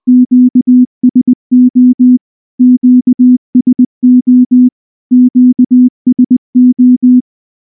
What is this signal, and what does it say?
Morse code, heard as a keyed tone, about 19 wpm, 250 Hz. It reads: QSO QSO QSO
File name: morse / morse-training-20WPM-2